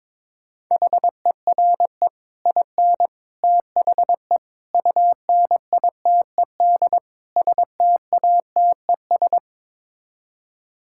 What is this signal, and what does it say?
Morse code, heard as a keyed tone, about 22 wpm, 700 Hz. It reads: HERE IN THE UNITED STATES
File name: practice / 22wpm / Build A Phrase - 5-Word Phrases - 22wpm